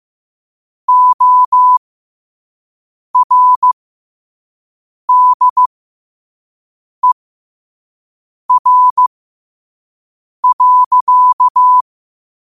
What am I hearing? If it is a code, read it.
Morse code, heard as a keyed tone, about 15 wpm, spaced as for 5 wpm, 1000 Hz. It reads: ORDER.
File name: qotdfiles / 5wpm